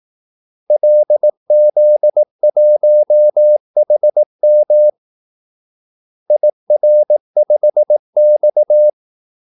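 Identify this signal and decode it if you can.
Morse code, heard as a keyed tone, about 18 wpm, 600 Hz. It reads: LZ1HM IR5X
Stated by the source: Morse call-sign practice at 18 wpm.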